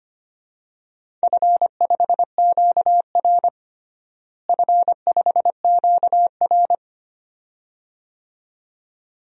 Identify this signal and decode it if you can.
Morse code, heard as a keyed tone, about 25 wpm, 700 Hz. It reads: F5QR F5QR
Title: DXCC-100 Call Signs - 4 Characters - Repeated 2x - 25wpm